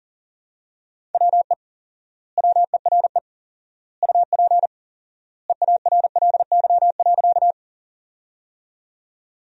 Morse code, heard as a keyed tone, about 40 wpm, 700 Hz. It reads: WE WERE UP EARLY.
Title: 4x Word-Spacing - Sentences from Top 400 Words - 40wpm